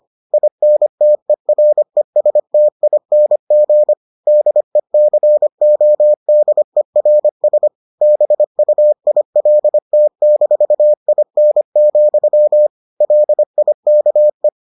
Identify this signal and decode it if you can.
Morse code, heard as a keyed tone, about 25 wpm, 600 Hz. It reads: INTERESTING DECODERS BUILT-IN, LIKE